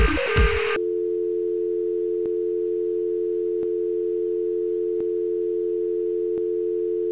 Dial tone (North America), 350 + 440 Hz.